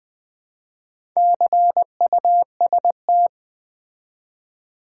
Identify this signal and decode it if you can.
Morse code, heard as a keyed tone, about 20 wpm, 700 Hz. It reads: CUST
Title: Build A Word - 2 Letters At A Time - 6-Letter Words - 20wpm